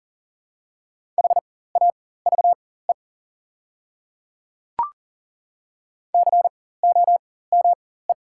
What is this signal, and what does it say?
Morse code, heard as a keyed tone, about 40 wpm, spaced as for 19 wpm, 690 Hz. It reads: HAVE COME
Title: 4x Character Spacing - Top 100 Words - 40wpm